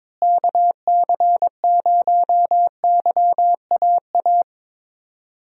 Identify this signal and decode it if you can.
Morse code, heard as a keyed tone, about 22 wpm, 700 Hz. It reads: KC0YAA
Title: USA Call Signs - 6 Characters - Repeated 3x - 22wpm